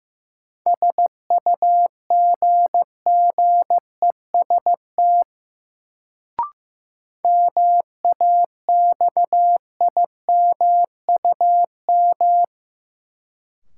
Morse code, heard as a keyed tone, about 15 wpm, 700 Hz. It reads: SUGGEST MAXIMUM
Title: Sentences from Top 1300 Words - New Word Drill - 15wpm